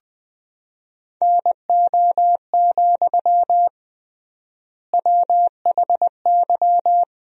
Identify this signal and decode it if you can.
Morse code, heard as a keyed tone, about 20 wpm, 700 Hz. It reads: NO, WHY
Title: 3x Word-Spacing - Sentences from Top 300 Words - 20wpm